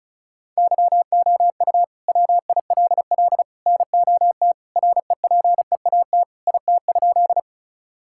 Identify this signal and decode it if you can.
Morse code, heard as a keyed tone, about 36 wpm, 700 Hz. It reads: YOU WILL NOT REPEAT IT?
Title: Sentences from Top 1600 Words 35wpm